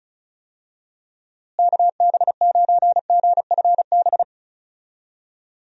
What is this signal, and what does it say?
Morse code, heard as a keyed tone, about 36 wpm, 700 Hz. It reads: KB9GFB
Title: USA Call Signs - 6 Characters - Repeated 2x - 35wpm